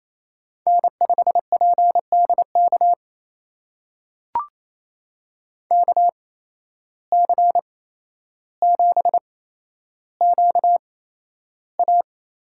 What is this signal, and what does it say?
Morse code, heard as a keyed tone, about 28 wpm, spaced as for 12 wpm, 700 Hz. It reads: N5PDK KC7QA
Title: USA Call Signs - 5 Characters - 8x Character Spacing - 28wpm